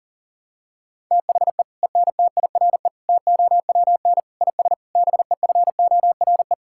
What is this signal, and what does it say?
Morse code, heard as a keyed tone, about 40 wpm, 700 Hz. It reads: THE ENTIRE TOWN IS BEFORE
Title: Sentences from Top 1000 Words - Review Part 5 - 40wpm